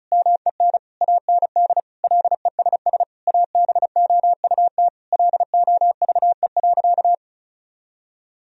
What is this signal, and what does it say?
Morse code, heard as a keyed tone, about 36 wpm, 700 Hz. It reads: MEN AND LESS ABOUT LOVE.
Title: Sentences from Top 1000 Words - Review Part 2 - 35wpm